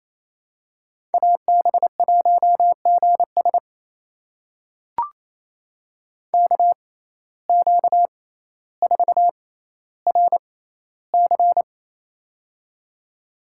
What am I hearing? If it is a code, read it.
Morse code, heard as a keyed tone, about 28 wpm, spaced as for 14 wpm, 700 Hz. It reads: AB1GS KQ4RC
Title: USA Call Signs - 5 Characters - 6x Character Spacing - 28wpm